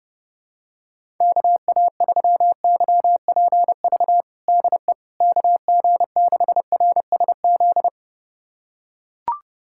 Morse code, heard as a keyed tone, about 30 wpm, 700 Hz. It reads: KA3YPV DE KG6RSZ
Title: USA Callsign DE Callsign 30wpm